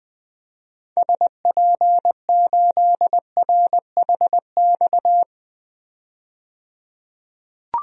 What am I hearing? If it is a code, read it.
Morse code, heard as a keyed tone, about 20 wpm, 700 Hz. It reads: SP8RHX
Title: DXCC-100 Call Signs - 6 Characters - 20wpm